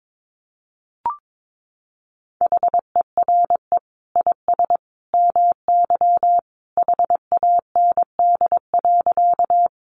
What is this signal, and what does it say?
Morse code, heard as a keyed tone, about 22 wpm, 700 Hz. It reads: HERE IS MY HAND.
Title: Sentences from Top 200 Words 22wpm